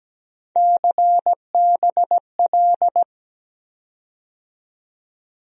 Morse code, heard as a keyed tone, about 17 wpm, 700 Hz. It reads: CBL
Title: USA Call Sign Postfixes - 3 Letters 17wpm